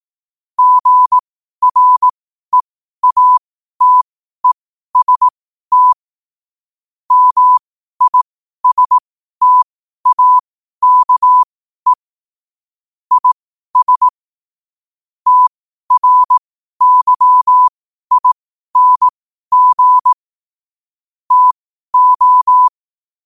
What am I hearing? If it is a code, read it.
Morse code, heard as a keyed tone, about 18 wpm, spaced as for 12 wpm, 1000 Hz. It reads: GREATEST MISTAKE IS TRYING TO
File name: qotdfiles / 12wpm